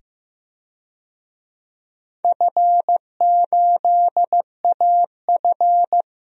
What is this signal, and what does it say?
Morse code, heard as a keyed tone, about 15 wpm, 700 Hz. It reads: F8AF
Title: DXCC-100 Call Signs - 4 Characters - 15wpm